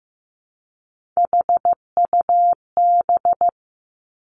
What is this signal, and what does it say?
Morse code, heard as a keyed tone, about 15 wpm, 700 Hz. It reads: HUB
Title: Three Challenging Letter-Number Combinations 15wpm